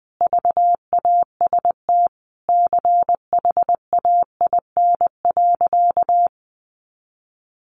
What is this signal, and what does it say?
Morse code, heard as a keyed tone, about 20 wpm, 700 Hz. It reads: VAST CHAIN.
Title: Sentences from Top 1500 Words 20wpm